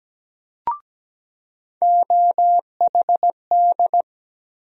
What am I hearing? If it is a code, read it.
Morse code, heard as a keyed tone, about 17 wpm, 700 Hz. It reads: OHD
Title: USA Call Sign Postfixes - 3 Letters 17wpm